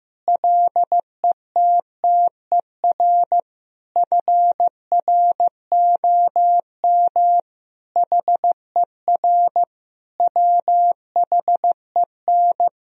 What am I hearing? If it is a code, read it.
Morse code, heard as a keyed tone, about 15 wpm, 700 Hz. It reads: LETTER FROM HER WHEN
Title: Sentences from Top 1000 Words - Review Part 3 - 15wpm